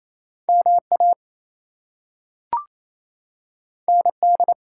Morse code, heard as a keyed tone, about 28 wpm, 700 Hz. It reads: MA ND
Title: US States and Canadian Provinces 28wpm